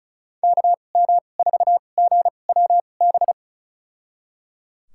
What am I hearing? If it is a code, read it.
Morse code, heard as a keyed tone, about 36 wpm, spaced as for 25 wpm, 700 Hz. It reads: KM4GWB
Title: USA Call Signs - 6 Characters - 2x Character Spacing - 35wpm